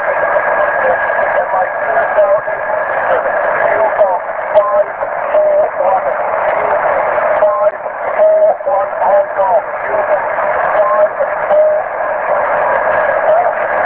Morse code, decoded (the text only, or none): none